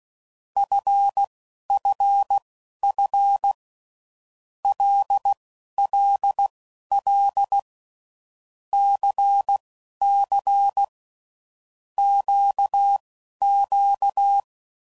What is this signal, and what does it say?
Morse code, heard as a keyed tone, about 16 wpm, spaced as for 11 wpm, 800 Hz. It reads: FFF LLL CC QQ
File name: Groups / Basic Letter Groups / LFCYQ